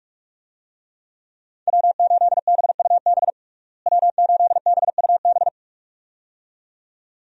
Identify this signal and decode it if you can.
Morse code, heard as a keyed tone, about 45 wpm, 700 Hz. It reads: W8BUB W8BUB
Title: USA Call Signs - 5 Characters - Repeated 2x - 45wpm